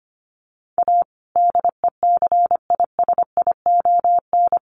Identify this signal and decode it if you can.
Morse code, heard as a keyed tone, about 25 wpm, 700 Hz. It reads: A DECISION
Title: Sentences from Top 1100 Words 25wpm